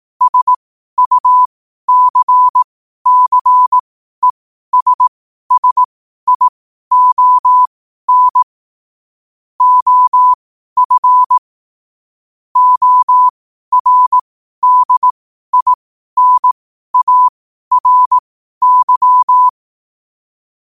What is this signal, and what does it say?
Morse code, heard as a keyed tone, about 18 wpm, spaced as for 12 wpm, 1000 Hz. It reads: SUCCESSION OF ORDINARY
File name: qotdfiles / 12wpm